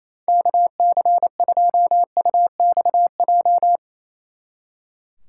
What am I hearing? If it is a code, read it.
Morse code, heard as a keyed tone, about 35 wpm, 700 Hz. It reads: KC2UXJ
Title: Speed Racing - USA Call Signs 28wpm